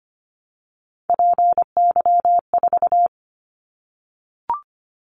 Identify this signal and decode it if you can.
Morse code, heard as a keyed tone, about 25 wpm, 700 Hz. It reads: PY4
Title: Callsign Prefixes for Top 50 DXCC - 25wpm